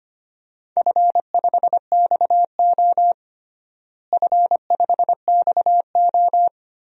Morse code, heard as a keyed tone, about 25 wpm, 700 Hz. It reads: F5XO F5XO
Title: DXCC-100 Call Signs - 4 Characters - Repeated 2x - 25wpm